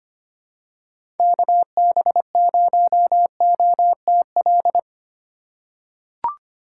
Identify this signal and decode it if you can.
Morse code, heard as a keyed tone, about 25 wpm, 700 Hz. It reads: KB0OTL
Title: USA Call Signs - 6 Characters - 2x Character Spacing - 25wpm